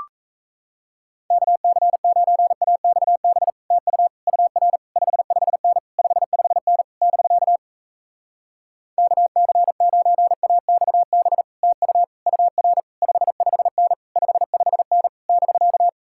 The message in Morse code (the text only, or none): KC9AXB TU UR 55N 55N <BK> KC9AXB TU UR 55N 55N <BK>